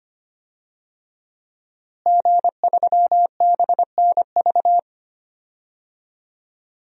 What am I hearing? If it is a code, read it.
Morse code, heard as a keyed tone, about 25 wpm, 700 Hz. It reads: G3BNV
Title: DXCC-100 Call Signs - 5 Characters - Repeated 3x - 25wpm